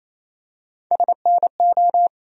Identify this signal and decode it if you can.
Morse code, heard as a keyed tone, about 28 wpm, 700 Hz. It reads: SNO
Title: USA Call Sign Postfixes - 3 Letters 28wpm